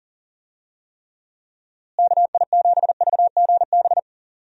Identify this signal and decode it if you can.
Morse code, heard as a keyed tone, about 40 wpm, 700 Hz. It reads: KI7VGB
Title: USA Call Signs - 6 Characters - Repeated 2x - 40wpm